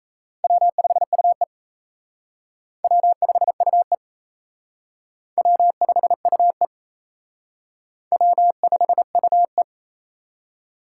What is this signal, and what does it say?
Morse code, heard as a keyed tone, about 36 wpm, 700 Hz. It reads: W5UE W5UE W5UE W5UE